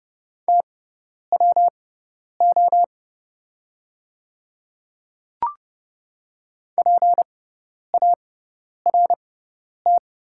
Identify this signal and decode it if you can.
Morse code, heard as a keyed tone, about 30 wpm, spaced as for 10 wpm, 700 Hz. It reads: TWO PART
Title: 6x Character Spacing - Top 200 Words - 30wpm